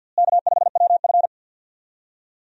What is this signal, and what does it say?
Morse code, heard as a keyed tone, about 50 wpm, 700 Hz. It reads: K5WF
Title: USA Call Signs - 4 Characters - Repeated 3x - 50wpm